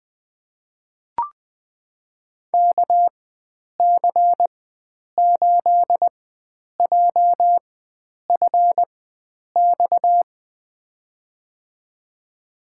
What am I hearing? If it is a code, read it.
Morse code, heard as a keyed tone, about 20 wpm, spaced as for 9 wpm, 700 Hz. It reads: KC8JFX
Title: USA Call Signs - 6 Characters - 4x Character Spacing - 20wpm